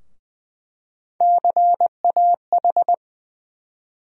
Morse code, heard as a keyed tone, about 21 wpm, 700 Hz. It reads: CAH
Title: USA Call Sign Postfixes - 3 Letters 20wpm